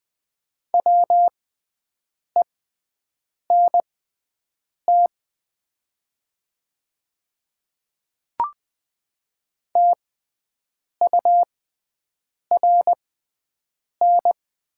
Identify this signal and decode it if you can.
Morse code, heard as a keyed tone, about 20 wpm, spaced as for 7 wpm, 700 Hz. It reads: WENT TURN
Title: 6x Character Spacing - Top 200 Words - 20wpm